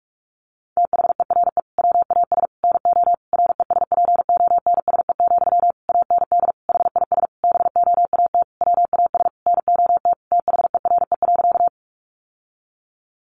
Morse code, heard as a keyed tone, about 44 wpm, 700 Hz. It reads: THERE WAS NO RESPONSE, AND HIS BOAT WAS NOT THERE.
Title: Sentences from Top 1400 Words 45wpm